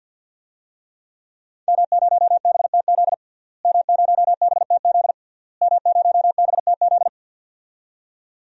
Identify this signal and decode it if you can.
Morse code, heard as a keyed tone, about 46 wpm, 700 Hz. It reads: M0BTZ M0BTZ M0BTZ
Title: DXCC-100 Call Signs - 5 Characters - Repeated 3x - 50wpm